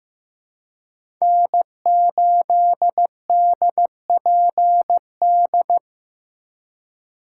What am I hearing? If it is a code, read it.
Morse code, heard as a keyed tone, about 15 wpm, 700 Hz. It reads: N8DPD
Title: K1USN SST QSO Practice - 15wpm